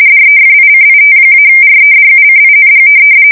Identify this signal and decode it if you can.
RTTY, about 45.45 baud. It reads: DEFICIT-REDUCING
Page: 170 Shift 2125/2295